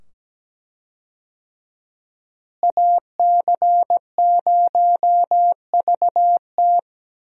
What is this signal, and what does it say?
Morse code, heard as a keyed tone, about 17 wpm, 700 Hz. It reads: AC0VT
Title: USA Call Signs - 5 Characters - Repeated 2x - 17wpm